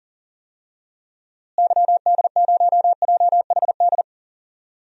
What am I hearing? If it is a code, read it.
Morse code, heard as a keyed tone, about 40 wpm, 700 Hz. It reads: YD0JHD
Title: DXCC-100 Call Signs - 6 Characters - Repeated 2x - 40wpm